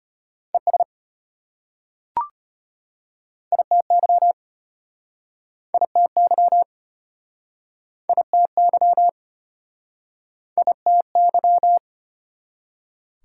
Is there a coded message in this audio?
Yes: Morse code from 0.5 to 11.8 s, heard as a keyed tone, about 33 wpm, 700 Hz. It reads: ES ITY ITY ITY ITY